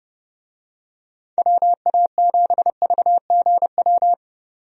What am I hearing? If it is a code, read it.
Morse code, heard as a keyed tone, about 30 wpm, 700 Hz. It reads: WA7VGW
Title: USA Call Signs - 6 Characters - 30wpm